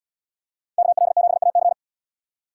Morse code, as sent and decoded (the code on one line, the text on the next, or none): -.- .-.. --... .- -.-.
KL7AC